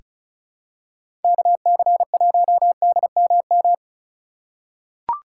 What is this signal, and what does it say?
Morse code, heard as a keyed tone, about 36 wpm, 700 Hz. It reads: KC1DMM
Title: USA Call Signs - 6 Characters - 6x Character Spacing - 35wpm